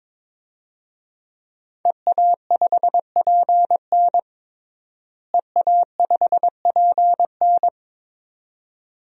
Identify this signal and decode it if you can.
Morse code, heard as a keyed tone, about 22 wpm, 700 Hz. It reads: EA5PN EA5PN
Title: DXCC-100 Call Signs - 5 Characters - Repeated 2x - 22wpm